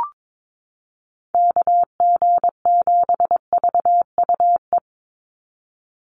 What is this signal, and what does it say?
Morse code, heard as a keyed tone, about 22 wpm, 700 Hz. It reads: KG7VUE
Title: USA Call Signs 22wpm